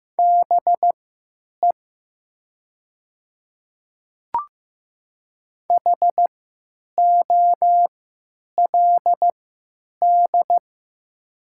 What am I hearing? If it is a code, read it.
Morse code, heard as a keyed tone, about 15 wpm, spaced as for 9 wpm, 700 Hz. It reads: BE HOLD
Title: 3x Character Spacing - Top 500 Words - 15wpm